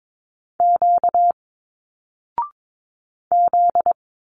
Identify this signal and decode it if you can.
Morse code, heard as a keyed tone, about 22 wpm, 700 Hz. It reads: Q Z
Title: Single Letters 22wpm